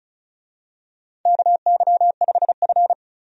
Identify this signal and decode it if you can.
Morse code, heard as a keyed tone, about 36 wpm, 700 Hz. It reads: KY5F
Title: USA Call Signs - 4 Characters - 35wpm